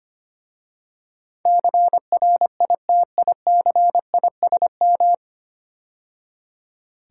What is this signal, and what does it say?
Morse code, heard as a keyed tone, about 25 wpm, 700 Hz. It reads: CRITICISM
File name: practice / 25wpm / 9 Letter Words - First 3 Letters Given - 25wpm